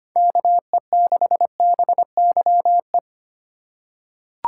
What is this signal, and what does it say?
Morse code, heard as a keyed tone, about 25 wpm, 700 Hz. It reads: KE6BYE
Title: USA Call Signs - 6 Characters - 4x Character Spacing - 25wpm